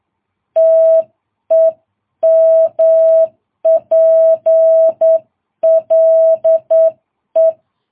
Morse code, heard as a keyed tone, about 8 wpm, 660 Hz. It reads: TEMPLE
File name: jh8cbh